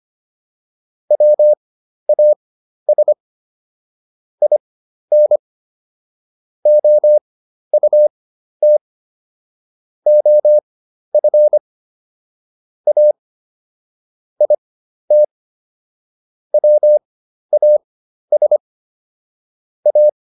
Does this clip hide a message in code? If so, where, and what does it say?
Yes: Morse code from 1.1 to 20.1 s, heard as a keyed tone, about 25 wpm, spaced as for 12 wpm, 600 Hz. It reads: WAS IN OUT OF A IT WAS A